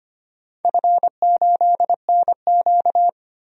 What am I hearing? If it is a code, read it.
Morse code, heard as a keyed tone, about 25 wpm, 700 Hz. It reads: F8NQ
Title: DXCC-100 Call Signs - 4 Characters - 25wpm